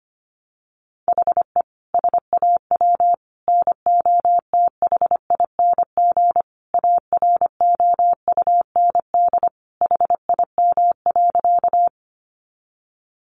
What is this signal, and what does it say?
Morse code, heard as a keyed tone, about 25 wpm, 700 Hz. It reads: HE SAW NOTHING AROUND HIM.
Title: Sentences from Top 600 Words 25wpm